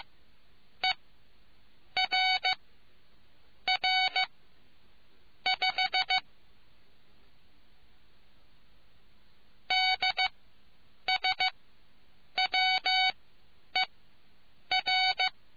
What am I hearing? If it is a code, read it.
Morse code, heard as a keyed tone, about 15 wpm, spaced as for 7 wpm, 2250 Hz. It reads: ERR5 DSWER